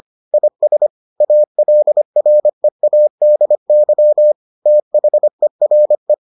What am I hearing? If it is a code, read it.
Morse code, heard as a keyed tone, about 25 wpm, 600 Hz. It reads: IS ALREADY THERE